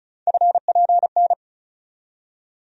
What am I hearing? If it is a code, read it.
Morse code, heard as a keyed tone, about 34 wpm, 700 Hz. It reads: FPN
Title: USA Call Sign Postfixes - 3 Letters 35wpm